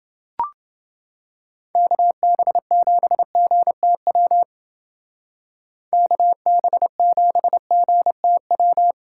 Morse code, heard as a keyed tone, about 29 wpm, 700 Hz. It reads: KB7GTW KB7GTW